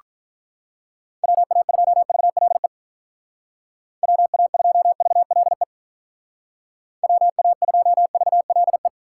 Morse code, heard as a keyed tone, about 44 wpm, 700 Hz. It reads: WA2VLE WA2VLE WA2VLE